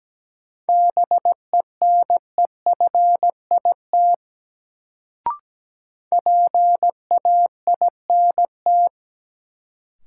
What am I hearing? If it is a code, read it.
Morse code, heard as a keyed tone, about 17 wpm, 700 Hz. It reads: BENEFIT PAINT
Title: Sentences from Top 1000 Words - Review Part 6 - New Word Drill - 17wpm